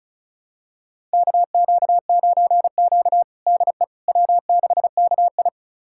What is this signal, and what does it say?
Morse code, heard as a keyed tone, about 34 wpm, 700 Hz. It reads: KQ9Q DE W6KI
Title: USA Callsign DE Callsign - 4 Characters - 35wpm